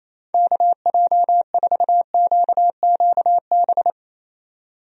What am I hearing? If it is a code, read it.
Morse code, heard as a keyed tone, about 28 wpm, 700 Hz. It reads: KJ4QQB
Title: USA Call Signs - 6 Characters - 28wpm